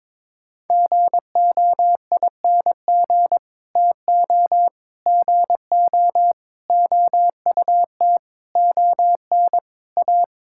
Morse code, heard as a keyed tone, about 22 wpm, 700 Hz. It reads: GOING TO GO OUT ON A
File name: practice / 22wpm / Sets of 8 Words - 22wpm